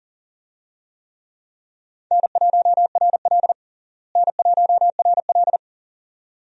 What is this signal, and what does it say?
Morse code, heard as a keyed tone, about 40 wpm, 700 Hz. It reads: N1RL N1RL
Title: USA Call Signs - 4 Characters - Repeated 2x - 40wpm